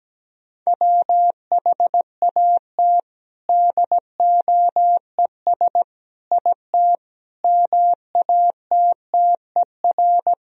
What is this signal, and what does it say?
Morse code, heard as a keyed tone, about 17 wpm, 700 Hz. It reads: WHAT DOES IT MATTER
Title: Sentences from Top 1000 Words - Review Part 3 - 17wpm